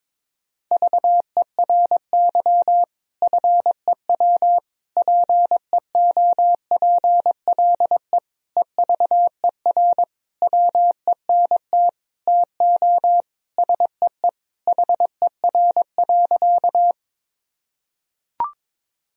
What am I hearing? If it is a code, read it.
Morse code, heard as a keyed tone, about 21 wpm, 700 Hz. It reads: VERY FEW PEOPLE EVER WENT TO SEE HER.
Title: Sentences from Top 1000 Words - Review Part 1 - 22wpm